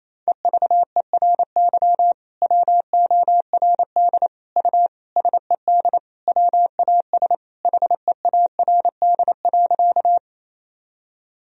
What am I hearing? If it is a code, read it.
Morse code, heard as a keyed tone, about 28 wpm, 700 Hz. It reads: EVERY WORD U SED WAS HEARD.
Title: Sentences from Top 400 Words and Abbreviations 28wpm